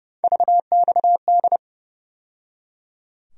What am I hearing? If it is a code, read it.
Morse code, heard as a keyed tone, about 30 wpm, 700 Hz. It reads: VXD
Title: USA Call Sign Postfixes - 3 Letters 30wpm